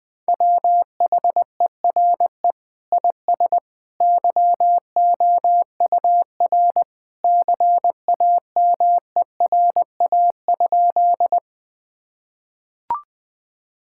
Morse code, heard as a keyed tone, about 20 wpm, 700 Hz. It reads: WHERE IS YOUR CAMERA?
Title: Sentences from Top 1000 Words - Review Part 5 - 20wpm